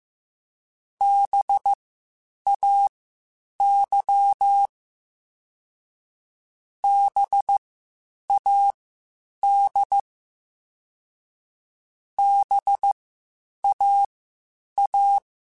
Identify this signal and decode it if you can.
Morse code, heard as a keyed tone, about 15 wpm, spaced as for 8 wpm, 800 Hz. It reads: BAY BAD BAA